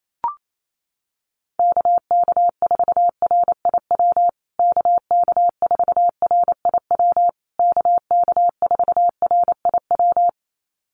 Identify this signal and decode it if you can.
Morse code, heard as a keyed tone, about 28 wpm, 700 Hz. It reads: KK4RIW KK4RIW KK4RIW
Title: USA Call Signs Repeated 3x 28wpm